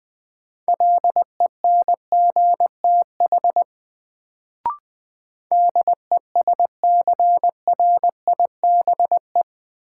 Morse code, heard as a keyed tone, about 20 wpm, 700 Hz. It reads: LENGTH DESCRIBE
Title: Sentences from Top 1000 Words - New Word Drill - 20wpm